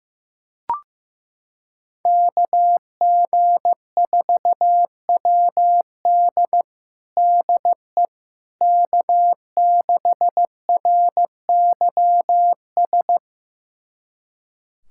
Morse code, heard as a keyed tone, about 15 wpm, 700 Hz. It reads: KG4WD DE K6RYS
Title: USA Callsign DE Callsign - 5 Characters - 15wpm